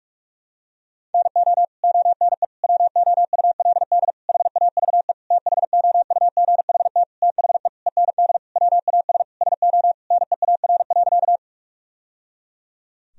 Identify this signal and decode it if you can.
Morse code, heard as a keyed tone, about 42 wpm, 700 Hz. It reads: NO ONE WOULD HAVE THOUGHT THE END WAS SO NEAR.
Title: Sentences from Top 1000 Words - Review Part 1 - 45wpm